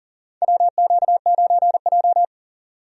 Morse code, heard as a keyed tone, about 40 wpm, 700 Hz. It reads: WQ9J
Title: USA Call Signs - 4 Characters - 40wpm